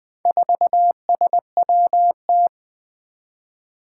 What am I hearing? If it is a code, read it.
Morse code, heard as a keyed tone, about 20 wpm, 700 Hz. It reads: 4SWT
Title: Four Challenging Letter-Number Combinations 20wpm